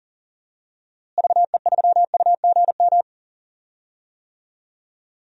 Morse code, heard as a keyed tone, about 40 wpm, 700 Hz. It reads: VE3UGM
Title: DXCC-100 Call Signs - 6 Characters - Repeated 2x - 40wpm